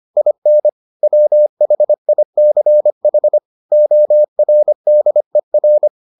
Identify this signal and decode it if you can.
Morse code, heard as a keyed tone, about 25 wpm, 600 Hz. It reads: IN WHICH ORDER